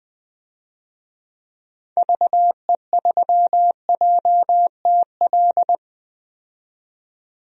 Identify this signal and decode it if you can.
Morse code, heard as a keyed tone, about 21 wpm, 700 Hz. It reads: VE3JTL
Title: DXCC-100 Call Signs - 6 Characters - 20wpm